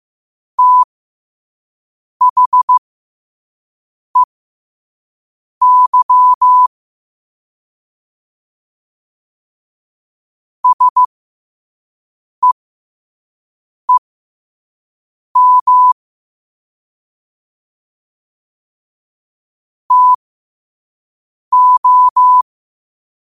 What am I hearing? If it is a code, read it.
Morse code, heard as a keyed tone, about 15 wpm, spaced as for 5 wpm, 1000 Hz. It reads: THEY SEEM TO
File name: qotdfiles / 5wpm